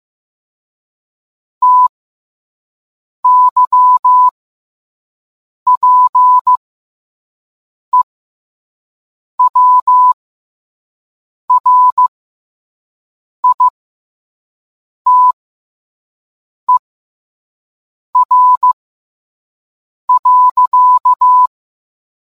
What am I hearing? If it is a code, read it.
Morse code, heard as a keyed tone, about 15 wpm, spaced as for 5 wpm, 1000 Hz. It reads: TYPEWRITER.